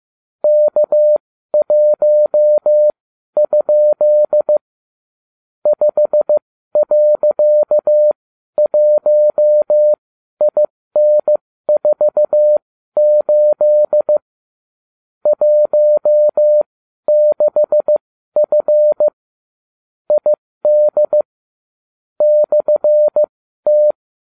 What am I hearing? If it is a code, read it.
Morse code, heard as a keyed tone, about 15 wpm, spaced as for 12 wpm, 600 Hz. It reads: K1? 5.1IN48 16F ID /T